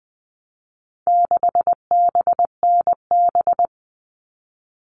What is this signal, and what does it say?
Morse code, heard as a keyed tone, about 20 wpm, 700 Hz. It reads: 6BNB
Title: All Cut Numbers - 4-Digit 20wpm